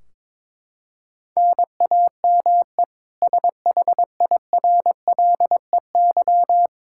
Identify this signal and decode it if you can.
Morse code, heard as a keyed tone, about 22 wpm, 700 Hz. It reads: NAME SHIRLEY
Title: K1USN SST QSO Practice 22wpm